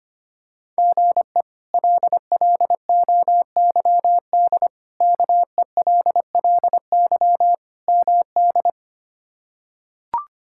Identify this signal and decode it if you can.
Morse code, heard as a keyed tone, about 25 wpm, 700 Hz. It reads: GE LLOYD KELLY MD
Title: K1USN SST QSO Practice - 25wpm